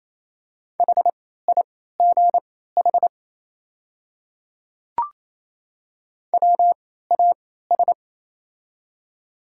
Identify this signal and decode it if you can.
Morse code, heard as a keyed tone, about 28 wpm, spaced as for 16 wpm, 700 Hz. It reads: HIGH WAS
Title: 3x Character Spacing - Top 200 Words - 28wpm